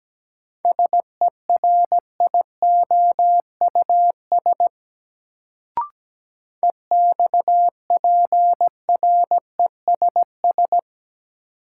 Morse code, heard as a keyed tone, about 17 wpm, 700 Hz. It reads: SERIOUS EXPRESS
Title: Sentences from Top 1100 Words - New Word Drill - 17wpm